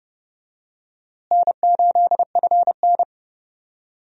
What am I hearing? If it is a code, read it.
Morse code, heard as a keyed tone, about 30 wpm, 700 Hz. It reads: N8FN
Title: USA Call Signs - 4 Characters - Repeated 2x - 30wpm